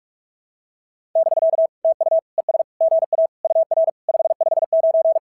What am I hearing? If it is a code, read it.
Morse code, heard as a keyed tone, about 42 wpm, 660 Hz. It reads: <BK> TU ES GA UR 559
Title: SOTA QSO Practice 45wpm